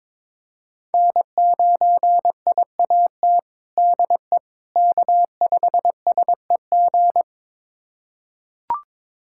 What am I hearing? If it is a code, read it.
Morse code, heard as a keyed tone, about 22 wpm, 700 Hz. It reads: N9IAT DE K5SEG
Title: USA Callsign DE Callsign - 5 Characters - 22wpm